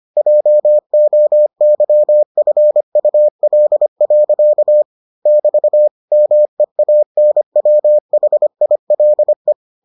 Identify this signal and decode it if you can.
Morse code, heard as a keyed tone, about 25 wpm, 600 Hz. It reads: JOYFUL. <BT> MEANWHILE